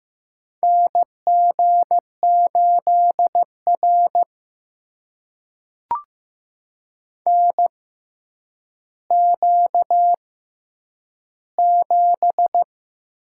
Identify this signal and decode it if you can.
Morse code, heard as a keyed tone, about 15 wpm, spaced as for 9 wpm, 700 Hz. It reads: NG8R NQ7
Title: USA Call Signs - 4 Characters - 6x Character Spacing - 15wpm